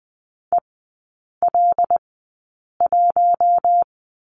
Morse code, heard as a keyed tone, about 20 wpm, spaced as for 8 wpm, 700 Hz. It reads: EL1
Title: 159 Lesson 26e - Letter g - Three Characters 20wpm